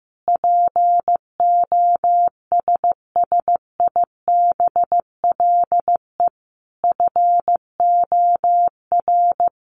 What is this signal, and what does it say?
Morse code, heard as a keyed tone, about 15 wpm, 700 Hz. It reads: POSSIBLE FOR
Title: Sentences from Top 900 Words 15wpm